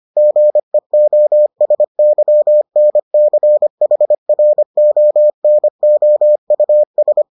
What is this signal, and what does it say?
Morse code, heard as a keyed tone, about 25 wpm, 600 Hz. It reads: GEOSYNCHRONOUS